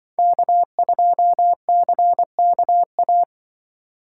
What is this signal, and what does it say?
Morse code, heard as a keyed tone, about 24 wpm, 700 Hz. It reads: K2CKA
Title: Speed Racing - USA Call Signs 20wpm